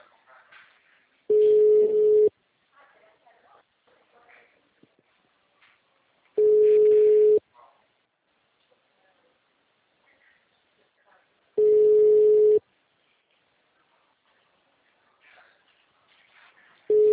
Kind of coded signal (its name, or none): telephone call progress tone